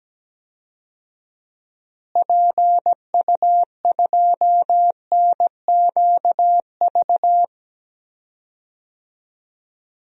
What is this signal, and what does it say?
Morse code, heard as a keyed tone, about 17 wpm, 700 Hz. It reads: PU2NQV
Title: DXCC-100 Call Signs - 6 Characters - Repeated 3x - 17wpm